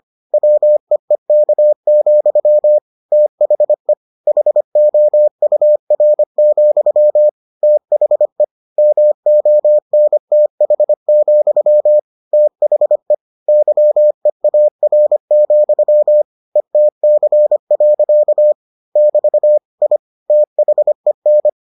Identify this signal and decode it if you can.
Morse code, heard as a keyed tone, about 25 wpm, 600 Hz. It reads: WEEK, THE HOUR, THE MONTH, THE YEAR, ETC. <BT> I THEN